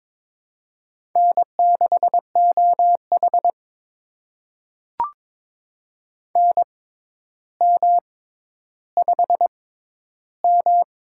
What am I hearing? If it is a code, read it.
Morse code, heard as a keyed tone, about 22 wpm, spaced as for 11 wpm, 700 Hz. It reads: N6OH NM5M
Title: USA Call Signs - 4 Characters - 6x Character Spacing - 22wpm